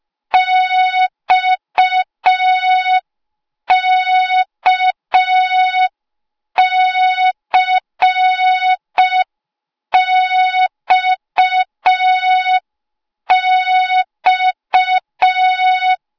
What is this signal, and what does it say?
Morse code, heard as a keyed tone, about 5 wpm, 3700 Hz. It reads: XKCXX
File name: Morze-HA2MN / hang / 08KXC HA2MN